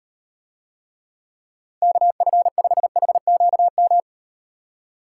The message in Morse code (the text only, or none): KF5HQM